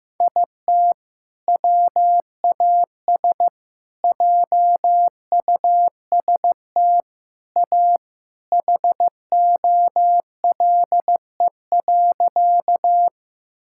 Morse code, heard as a keyed tone, about 15 wpm, 700 Hz. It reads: IT WAS JUST A HOLE.
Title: Sentences from Top 1600 Words 15wpm